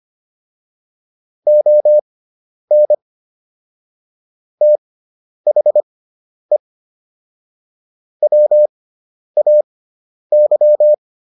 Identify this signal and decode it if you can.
Morse code, heard as a keyed tone, about 25 wpm, spaced as for 10 wpm, 600 Hz. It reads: ON THE WAY